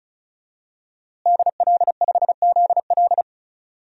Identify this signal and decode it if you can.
Morse code, heard as a keyed tone, about 36 wpm, 700 Hz. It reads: DL5ZL
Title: DXCC-100 Call Signs - 5 Characters - Repeated 3x - 35wpm